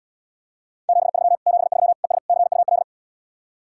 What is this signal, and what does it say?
Morse code, heard as a keyed tone, about 77 wpm, 700 Hz. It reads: 73 73 ES CUL